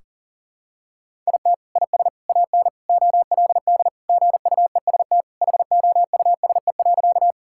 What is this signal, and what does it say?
Morse code, heard as a keyed tone, about 40 wpm, 700 Hz. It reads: IT IS AN OLD GUEST HOUSE.
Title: Sentences from Top 1600 Words 40wpm